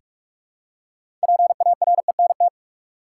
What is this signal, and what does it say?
Morse code, heard as a keyed tone, about 44 wpm, 700 Hz. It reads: PARENT
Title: Six Letter Words - 45wpm